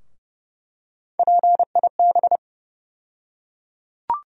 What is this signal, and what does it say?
Morse code, heard as a keyed tone, about 30 wpm, 700 Hz. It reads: PIB
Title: USA Call Sign Postfixes - 3 Letters 30wpm